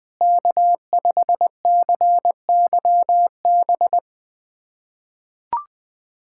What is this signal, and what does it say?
Morse code, heard as a keyed tone, about 20 wpm, 700 Hz. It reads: K5CYB
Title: USA Callsign DE Callsign - 5 Characters - 20wpm